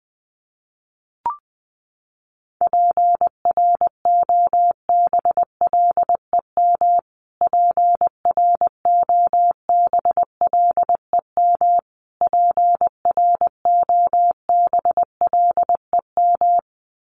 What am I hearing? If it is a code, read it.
Morse code, heard as a keyed tone, about 20 wpm, 700 Hz. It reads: PROBLEM PROBLEM PROBLEM
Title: Top 500 Words Repeated 20wpm